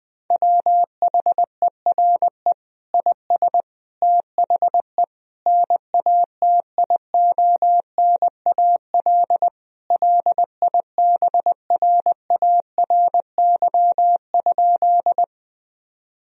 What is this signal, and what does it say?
Morse code, heard as a keyed tone, about 20 wpm, 700 Hz. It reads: WHERE IS THE NATIONAL LIBRARY?
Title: Sentences from Top 1000 Words - Review Part 4 - 20wpm